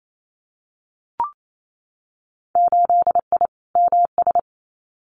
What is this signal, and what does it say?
Morse code, heard as a keyed tone, about 28 wpm, 700 Hz. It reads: 8I MS
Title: Winter Field Day Exchanges Encore 28wpm